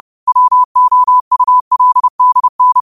Morse code, heard as a keyed tone, about 30 wpm, 1000 Hz. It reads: WOULDN
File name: qotdfiles / 30wpm